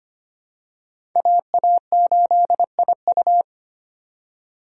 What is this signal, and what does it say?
Morse code, heard as a keyed tone, about 25 wpm, 700 Hz. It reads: AA8IU
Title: USA Call Signs - 2x Character Spacing - 25wpm